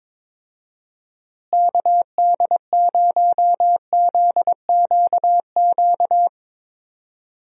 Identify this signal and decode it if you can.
Morse code, heard as a keyed tone, about 21 wpm, 700 Hz. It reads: KD0ZQQ
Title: USA Call Signs - 6 Characters - Repeated 2x - 22wpm